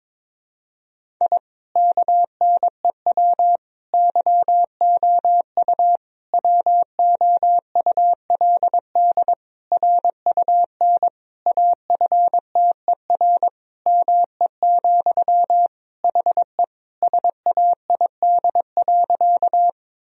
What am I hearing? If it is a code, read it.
Morse code, heard as a keyed tone, about 22 wpm, 700 Hz. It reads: I KNEW YOU WOULD RUN AFTER ME, HE SAID.
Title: Sentences from Top 1000 Words - Review Part 1 - 22wpm